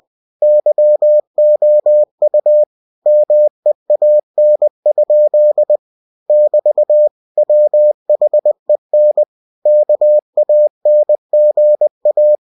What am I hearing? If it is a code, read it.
Morse code, heard as a keyed tone, about 20 wpm, 600 Hz. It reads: YOU MEAN? <BT> WHEN KANGA
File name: cw-books / winnie-the-pooh / winnie-20wpm-page-0064